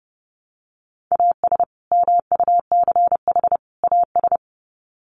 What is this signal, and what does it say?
Morse code, heard as a keyed tone, about 30 wpm, 700 Hz. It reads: AS MUCH AS
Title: Sets of 3 Words 30wpm